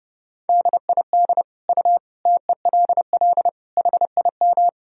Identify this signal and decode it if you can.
Morse code, heard as a keyed tone, about 30 wpm, 700 Hz. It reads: DID U TELL HIM
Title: Sentences from Top 400 Words and Abbreviations 30wpm